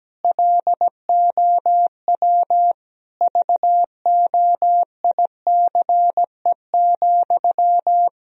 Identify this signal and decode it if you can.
Morse code, heard as a keyed tone, about 17 wpm, 700 Hz. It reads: LOW VOICE,
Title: Sentences from Top 1000 Words - Review Part 1 - 17wpm